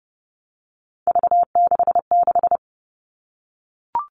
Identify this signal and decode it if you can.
Morse code, heard as a keyed tone, about 30 wpm, 700 Hz. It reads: V66
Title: All Cut Numbers - 3-Digit 30wpm